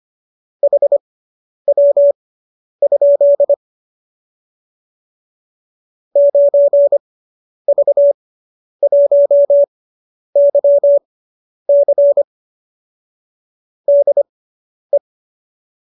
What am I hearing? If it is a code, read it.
Morse code, heard as a keyed tone, about 25 wpm, spaced as for 10 wpm, 570 Hz. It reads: HW? 9V1YC DE